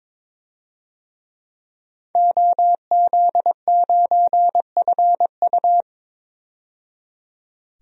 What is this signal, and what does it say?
Morse code, heard as a keyed tone, about 21 wpm, 700 Hz. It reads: OZ9FU
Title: DXCC-100 Call Signs - 5 Characters - 22wpm